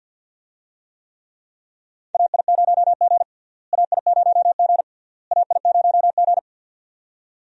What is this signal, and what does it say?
Morse code, heard as a keyed tone, about 52 wpm, 700 Hz. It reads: AI0G AI0G AI0G
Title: USA Call Signs - 4 Characters - Repeated 3x - 50wpm